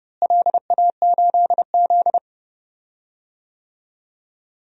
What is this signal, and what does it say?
Morse code, heard as a keyed tone, about 30 wpm, 700 Hz. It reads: LA8Z
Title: DXCC-100 Call Signs - 4 Characters - Repeated 2x - 30wpm